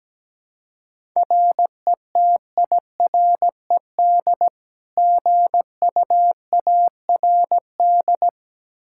Morse code, heard as a keyed tone, about 17 wpm, 700 Hz. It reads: RETIRED GUARD
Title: Practice QSO - Exchange 3 - More Info - 17wpm